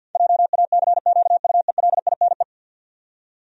Morse code, heard as a keyed tone, about 49 wpm, 700 Hz. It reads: JACQUELINE
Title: USA Common Names - 50wpm